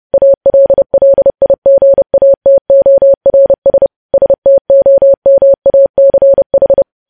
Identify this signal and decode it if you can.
Morse code, heard as a keyed tone, about 30 wpm, 550 Hz. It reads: ALLIGATORS STOMACH